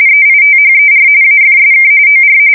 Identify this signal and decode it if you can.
RTTY, about 45.45 baud. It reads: EITHER AFSK OR
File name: fsk170 / teletype-RTTY